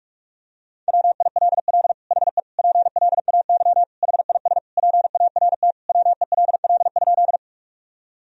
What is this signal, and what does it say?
Morse code, heard as a keyed tone, about 46 wpm, 700 Hz. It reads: WILL HE PLAY HIS PART WELL?
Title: Sentences from Top 1000 Words - Review Part 1 - 45wpm